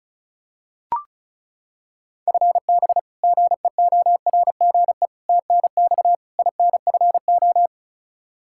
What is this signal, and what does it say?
Morse code, heard as a keyed tone, about 36 wpm, 700 Hz. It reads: FB GEORGE TNX INFO
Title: K1USN SST QSO Practice 35wpm